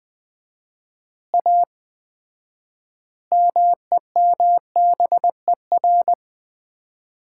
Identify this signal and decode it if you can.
Morse code, heard as a keyed tone, about 20 wpm, 700 Hz. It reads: A MEMBER
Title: 4x Word-Spacing - Sets of 5 Words - 20wpm